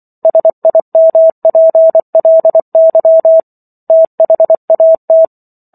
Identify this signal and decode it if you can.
Morse code, heard as a keyed tone, about 24 wpm, 650 Hz. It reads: SIMPLY THAT